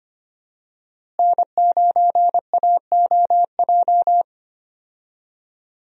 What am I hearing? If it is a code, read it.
Morse code, heard as a keyed tone, about 25 wpm, 700 Hz. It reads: N9AOJ
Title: USA Call Signs - 5 Characters - Repeated 3x - 25wpm